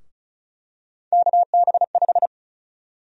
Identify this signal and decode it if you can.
Morse code, heard as a keyed tone, about 34 wpm, 700 Hz. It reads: KB5
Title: USA Call Sign Prefixes - Repeated 2x - 35wpm